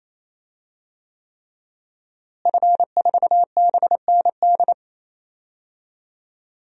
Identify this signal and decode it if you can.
Morse code, heard as a keyed tone, about 27 wpm, 700 Hz. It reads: F4BND
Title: DXCC-100 Call Signs - 5 Characters - Repeated 3x - 28wpm